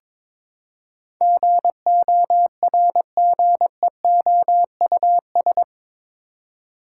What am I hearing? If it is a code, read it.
Morse code, heard as a keyed tone, about 22 wpm, 700 Hz. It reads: GORGEOUS
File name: practice / 22wpm / Eight Letter Words - 22wpm